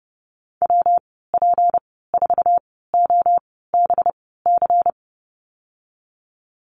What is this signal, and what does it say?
Morse code, heard as a keyed tone, about 29 wpm, spaced as for 17 wpm, 700 Hz. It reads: WP4OBC
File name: Lesson 37g - Letter x - Call Signs 30wpm